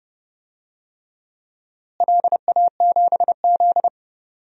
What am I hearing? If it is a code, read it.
Morse code, heard as a keyed tone, about 32 wpm, 700 Hz. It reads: LA7Z
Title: DXCC-100 Call Signs - 4 Characters - Repeated 3x - 30wpm